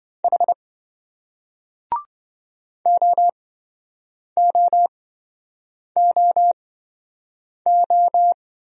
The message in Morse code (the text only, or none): H OOOO